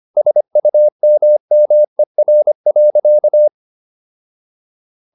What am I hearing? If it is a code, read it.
Morse code, heard as a keyed tone, about 25 wpm, 600 Hz. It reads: SUMMER.
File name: SS111 25